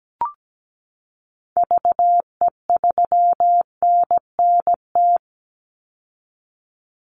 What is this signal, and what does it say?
Morse code, heard as a keyed tone, about 17 wpm, 700 Hz. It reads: VE3NNT
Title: Call Signs Repeated 3x - 17wpm